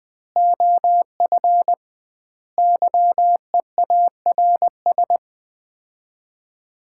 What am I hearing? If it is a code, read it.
Morse code, heard as a keyed tone, about 20 wpm, 700 Hz. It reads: OF YEARS
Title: 2x Word-Spacing - Sets of 5 Words - 20wpm